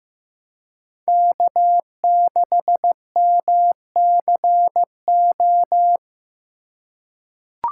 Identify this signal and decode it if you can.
Morse code, heard as a keyed tone, about 15 wpm, 700 Hz. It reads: K6MCO
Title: USA Call Signs - 2x Character Spacing - 15wpm